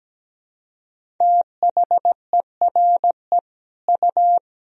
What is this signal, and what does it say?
Morse code, heard as a keyed tone, about 17 wpm, 700 Hz. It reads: THERE U
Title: Sentences from Top 100 Words and Abbreviations 17wpm